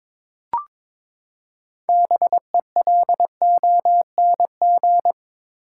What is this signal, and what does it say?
Morse code, heard as a keyed tone, about 22 wpm, 700 Hz. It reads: BELONG
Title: Six Letter Words - 22wpm